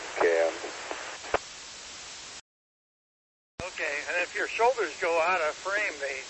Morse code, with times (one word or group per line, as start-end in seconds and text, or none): none